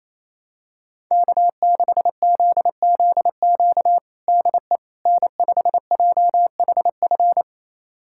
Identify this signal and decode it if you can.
Morse code, heard as a keyed tone, about 28 wpm, 700 Hz. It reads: K6ZZQ DE N5JHF
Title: USA Callsign DE Callsign - 5 Characters - 28wpm